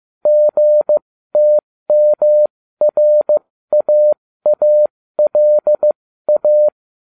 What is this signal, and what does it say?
Morse code, heard as a keyed tone, about 15 wpm, spaced as for 13 wpm, 600 Hz. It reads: GTMRAALA